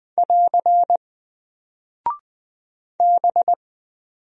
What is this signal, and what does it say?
Morse code, heard as a keyed tone, about 20 wpm, 700 Hz. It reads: <AR> B
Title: Single Character (Letter:Number:Punctuation:Prosign) Alt 20wpm